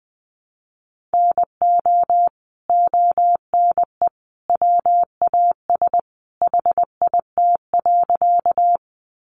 Morse code, heard as a keyed tone, about 20 wpm, 700 Hz. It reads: NO ONE WAS HIT.
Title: Sentences from Top 1200 Words 20wpm